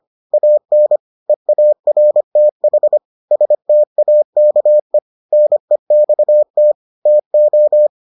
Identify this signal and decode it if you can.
Morse code, heard as a keyed tone, about 25 wpm, 600 Hz. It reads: AN EARTH STAKE NEXT TO